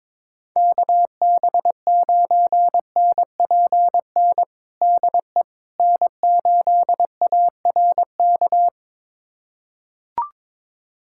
Morse code, heard as a keyed tone, about 22 wpm, 700 Hz. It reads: KB9NPN DE N8ARK
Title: USA Callsign DE Callsign 22wpm